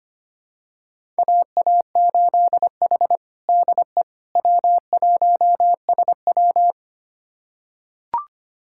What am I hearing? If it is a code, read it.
Morse code, heard as a keyed tone, about 25 wpm, 700 Hz. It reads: AA8H DE W1SW
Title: USA Callsign DE Callsign - 4 Characters - 25wpm